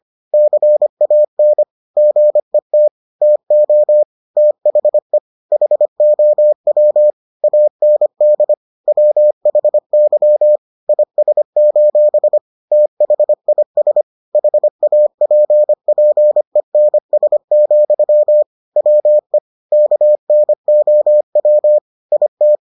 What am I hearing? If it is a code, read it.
Morse code, heard as a keyed tone, about 25 wpm, 600 Hz. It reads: CAN GET TO THE HOW AND WHY IS: THIS HAPPENS, WE KNOW IT